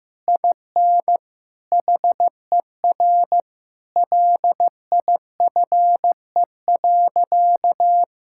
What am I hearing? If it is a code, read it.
Morse code, heard as a keyed tone, about 16 wpm, 700 Hz. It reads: IN HER LIFE.
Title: Sentences from Top 1000 Words - Review Part 6 - 15wpm